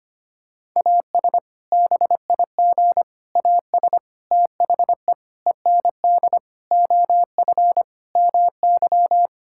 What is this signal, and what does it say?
Morse code, heard as a keyed tone, about 25 wpm, 700 Hz. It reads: AS BIG AS THE END OF MY
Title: Sentences from Top 1600 Words 25wpm